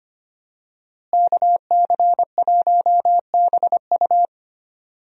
Morse code, heard as a keyed tone, about 25 wpm, 700 Hz. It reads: KC1BU
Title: USA Call Signs - 5 Characters - Repeated 3x - 25wpm